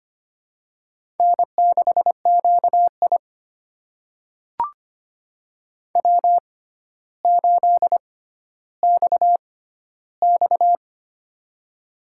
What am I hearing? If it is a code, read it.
Morse code, heard as a keyed tone, about 25 wpm, spaced as for 13 wpm, 700 Hz. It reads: N6QI W8XX
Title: USA Call Signs - 4 Characters - 6x Character Spacing - 25wpm